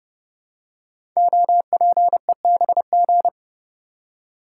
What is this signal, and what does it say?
Morse code, heard as a keyed tone, about 30 wpm, 700 Hz. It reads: OPEBG
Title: Code Groups - 5 Letter:Numbers - 30wpm